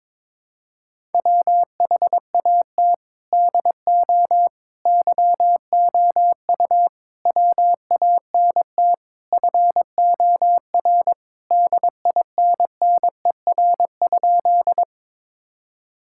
Morse code, heard as a keyed tone, about 22 wpm, 700 Hz. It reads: WHAT DO YOU WANT FOR DINNER?
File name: practice / 22wpm / Sentences with 4-Word Phrases - 22wpm